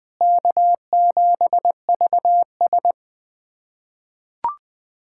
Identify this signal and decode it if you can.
Morse code, heard as a keyed tone, about 20 wpm, 700 Hz. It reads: K7VS
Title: USA Callsign DE Callsign - 4 Characters - 20wpm